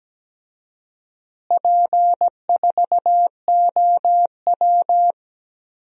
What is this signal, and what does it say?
Morse code, heard as a keyed tone, about 17 wpm, 700 Hz. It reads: P4OW
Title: DXCC-100 Call Signs - 4 Characters - Repeated 2x - 17wpm